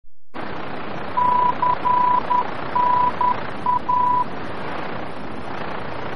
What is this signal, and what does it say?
Morse code, heard as a keyed tone, about 10 wpm, 990 Hz, with noise. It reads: CNA
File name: france / mx / CNA-VOR-F